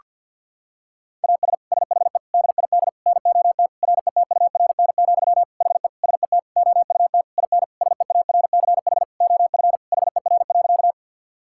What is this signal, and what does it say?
Morse code, heard as a keyed tone, about 46 wpm, 700 Hz. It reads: AS SHE DID NOT RETURN, HE SET OUT IN SEARCH OF HER.
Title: Sentences from Top 1000 Words - Review Part 5 - 50wpm